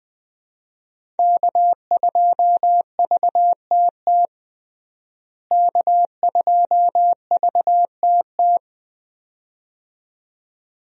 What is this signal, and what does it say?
Morse code, heard as a keyed tone, about 20 wpm, 700 Hz. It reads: K2VTT K2VTT
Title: USA Call Signs - 5 Characters - Repeated 2x - 20wpm